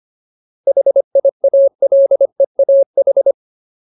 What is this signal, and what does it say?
Morse code, heard as a keyed tone, about 25 wpm, 550 Hz. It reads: HIALEAH
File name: Ultimate Word List / 25WPM Ultimate